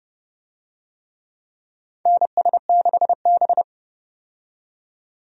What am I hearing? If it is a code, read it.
Morse code, heard as a keyed tone, about 29 wpm, 700 Hz. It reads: NS6B
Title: USA Call Signs - 4 Characters - 30wpm